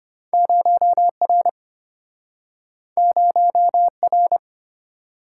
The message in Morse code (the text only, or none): 0R 0R